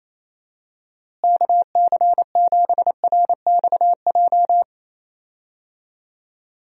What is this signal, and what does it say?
Morse code, heard as a keyed tone, about 28 wpm, 700 Hz. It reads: KC7RXJ
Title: USA Call Signs - 6 Characters - Repeated 3x - 28wpm